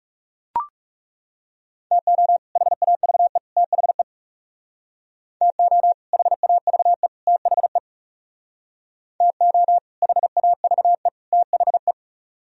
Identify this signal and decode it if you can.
Morse code, heard as a keyed tone, about 37 wpm, 700 Hz. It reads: TO HAVE THE TO HAVE THE TO HAVE THE
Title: Speed-Racing - Sets of 3 Words 30wpm